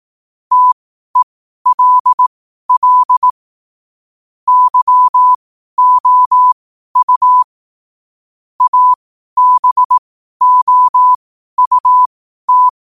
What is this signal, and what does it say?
Morse code, heard as a keyed tone, about 18 wpm, spaced as for 12 wpm, 1000 Hz. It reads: TELL YOU ABOUT
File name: qotdfiles / 12wpm